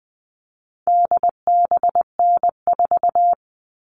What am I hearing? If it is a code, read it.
Morse code, heard as a keyed tone, about 20 wpm, 700 Hz. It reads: DBN4
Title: All Cut Numbers - 4-Digit 20wpm